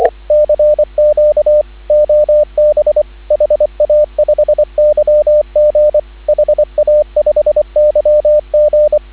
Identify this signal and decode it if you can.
Morse code, heard as a keyed tone, about 24 wpm, 590 Hz, with radio noise. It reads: CQ OB HA5YG HA5YG
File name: CW / 2017 HA-OB HA5YG